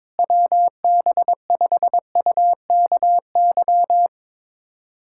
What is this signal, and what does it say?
Morse code, heard as a keyed tone, about 22 wpm, 700 Hz. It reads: WB5UKY
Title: USA Callsign DE Callsign - 6 Characters - 22wpm